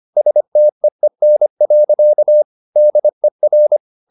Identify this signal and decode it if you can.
Morse code, heard as a keyed tone, about 25 wpm, 600 Hz. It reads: STEEN. DER
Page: Ferie 25 wpm